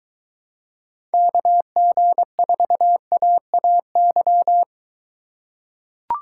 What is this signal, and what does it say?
Morse code, heard as a keyed tone, about 21 wpm, 700 Hz. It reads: KG4AAY